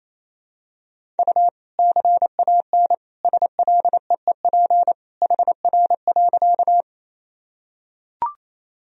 Morse code, heard as a keyed tone, about 28 wpm, 700 Hz. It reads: U CAN SLEEP HR.
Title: Sentences from Top 500 Words and Abbreviations 28wpm